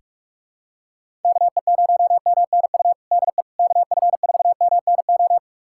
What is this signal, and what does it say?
Morse code, heard as a keyed tone, about 41 wpm, 700 Hz. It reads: KE0MNU DE KF4MNO
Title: USA Callsign DE Callsign - 6 Characters - 45wpm